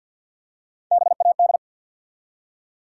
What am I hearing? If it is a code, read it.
Morse code, heard as a keyed tone, about 53 wpm, 700 Hz. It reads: BAD
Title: Three Letter Words - 50wpm